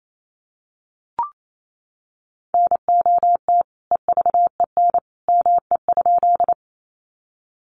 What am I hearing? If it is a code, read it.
Morse code, heard as a keyed tone, about 28 wpm, 700 Hz. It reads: NOT EVEN ME?
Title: Sentences from Top 600 Words 28wpm